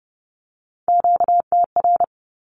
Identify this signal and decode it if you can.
Morse code, heard as a keyed tone, about 30 wpm, 700 Hz. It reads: QTR
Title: Q-Codes Repeated 3x 30wpm